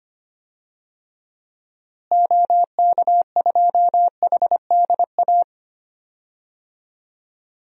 Morse code, heard as a keyed tone, about 25 wpm, 700 Hz. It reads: OK2HDA
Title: DXCC-100 Call Signs - 6 Characters - Repeated 3x - 25wpm